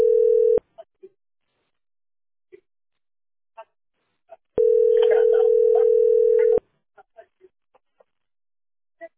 Ringback tone (North America), 440 + 480 Hz.